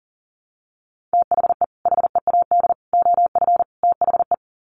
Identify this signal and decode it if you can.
Morse code, heard as a keyed tone, about 40 wpm, 700 Hz. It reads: THE HEAD OF THE
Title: Sets of 4 Words 40wpm